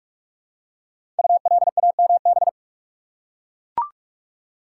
Morse code, heard as a keyed tone, about 44 wpm, 700 Hz. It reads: ULAMB
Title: Code Groups - 5 Letters - 45wpm